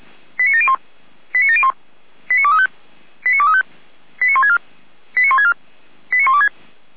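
Five-tone selective call (ZVEI), heard as quick runs of five tones, tones of about 70 ms. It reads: 89701 89701 89235 89235 89164 89164 89126
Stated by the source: ZVEI five-tone selective call received on SDR.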